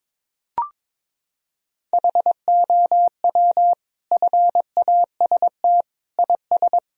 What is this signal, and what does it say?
Morse code, heard as a keyed tone, about 22 wpm, 700 Hz. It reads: HOW FAST IS
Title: Sentences from Top 1000 Words - Review Part 4 - 22wpm